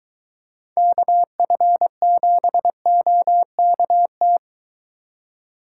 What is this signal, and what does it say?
Morse code, heard as a keyed tone, about 23 wpm, 700 Hz. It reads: KF7OKT